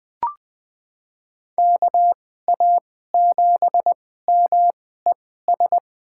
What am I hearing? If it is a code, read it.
Morse code, heard as a keyed tone, about 20 wpm, spaced as for 15 wpm, 700 Hz. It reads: KA7MES
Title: USA Call Signs - 6 Characters - 2x Character Spacing - 20wpm